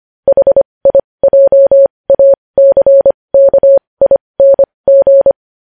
Morse code, heard as a keyed tone, about 25 wpm, spaced as for 20 wpm, 550 Hz. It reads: HIJACKING
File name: podcast / News headlines-1x-25x20